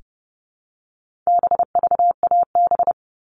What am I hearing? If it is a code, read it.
Morse code, heard as a keyed tone, about 30 wpm, 700 Hz. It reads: BVAB
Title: All Cut Numbers - 4-Digit 30wpm